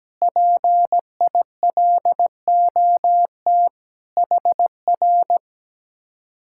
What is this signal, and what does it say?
Morse code, heard as a keyed tone, about 17 wpm, 700 Hz. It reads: PILOT HR
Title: Speed-Racing Common QSO Phrases - 17wpm